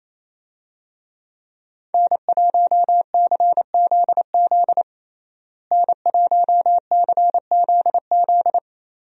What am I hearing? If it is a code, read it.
Morse code, heard as a keyed tone, about 27 wpm, 700 Hz. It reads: N1CZZ N1CZZ
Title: USA Call Signs - 5 Characters - Repeated 3x - 28wpm